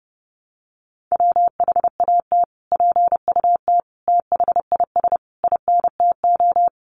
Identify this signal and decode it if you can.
Morse code, heard as a keyed tone, about 30 wpm, 700 Hz. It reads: WHAT PUT THIS INTO
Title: Sentences from Top 600 Words 30wpm